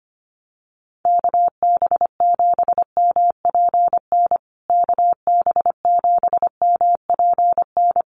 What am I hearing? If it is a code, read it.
Morse code, heard as a keyed tone, about 25 wpm, 700 Hz. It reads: KB7MPN KB7MPN
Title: USA Call Signs Repeated 3x 25wpm